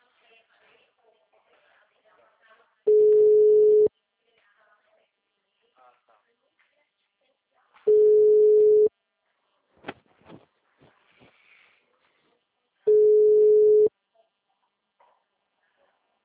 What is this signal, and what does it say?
Ringback tone (Europe, 425 Hz).
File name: out- -2006- - - .2279